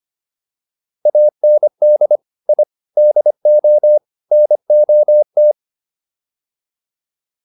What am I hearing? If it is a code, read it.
Morse code, heard as a keyed tone, about 25 wpm, 600 Hz. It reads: AND I DO NOT